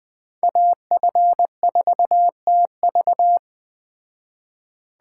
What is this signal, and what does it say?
Morse code, heard as a keyed tone, about 20 wpm, 700 Hz. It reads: AF4TV
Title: Speed Racing - USA 2x2 Call Signs 17wpm